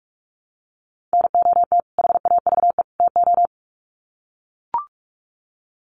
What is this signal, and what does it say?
Morse code, heard as a keyed tone, about 44 wpm, 700 Hz. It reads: NOT HAVE TO
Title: Sets of 3 Words 45wpm